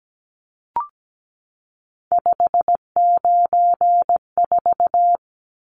Morse code, heard as a keyed tone, about 17 wpm, 700 Hz. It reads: 594
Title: Numbers - 3-Digit 17wpm